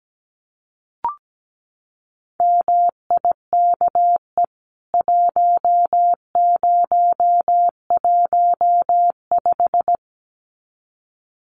Morse code, heard as a keyed tone, about 17 wpm, 700 Hz. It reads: MIKE 1015
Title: CWT Exchange 17wpm